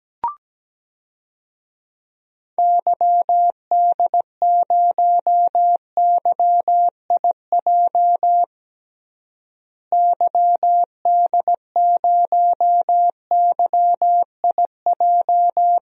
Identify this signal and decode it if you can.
Morse code, heard as a keyed tone, about 17 wpm, 700 Hz. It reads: YD0YIJ YD0YIJ
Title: DXCC-100 Call Signs - 6 Characters - Repeated 3x - 17wpm